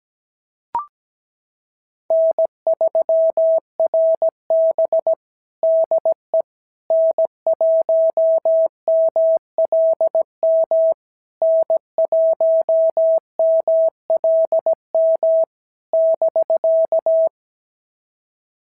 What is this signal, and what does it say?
Morse code, heard as a keyed tone, about 17 wpm, 650 Hz. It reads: N3RB DE N1MLM N1MLM <BK>
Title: SOTA QSO Practice 17wpm